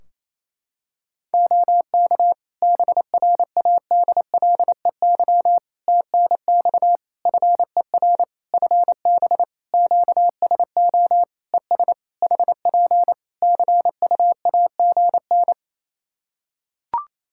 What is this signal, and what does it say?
Morse code, heard as a keyed tone, about 27 wpm, 700 Hz. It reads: OK BRADLEY TNX FER FB QSO ES HP CUAGN
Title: Practice QSO - Exchange 4 - Ending - 28wpm